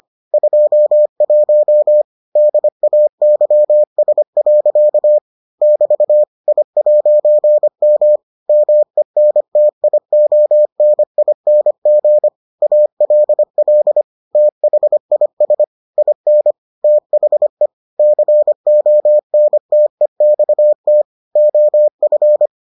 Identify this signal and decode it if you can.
Morse code, heard as a keyed tone, about 25 wpm, 600 Hz. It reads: 21 DAYS. <BT> I'M MENTIONING ALL THIS IN THE CONTEXT OF